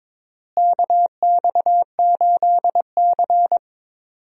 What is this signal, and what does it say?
Morse code, heard as a keyed tone, about 22 wpm, 700 Hz. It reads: KX8C
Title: USA Call Signs - 4 Characters - Repeated 2x - 22wpm